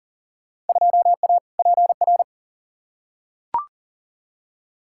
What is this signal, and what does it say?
Morse code, heard as a keyed tone, about 40 wpm, 700 Hz. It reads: 2A PR
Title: Field Day Exchanges Encore - 40wpm